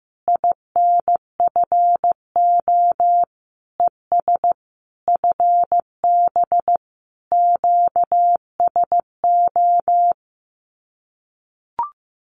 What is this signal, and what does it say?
Morse code, heard as a keyed tone, about 15 wpm, 700 Hz. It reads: INFO ES FB QSO
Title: Common QSO Phrases Encore 15wpm